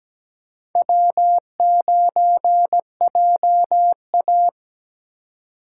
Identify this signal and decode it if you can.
Morse code, heard as a keyed tone, about 17 wpm, 700 Hz. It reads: W9JA
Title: USA Call Signs - 4 Characters - 2x Character Spacing - 17wpm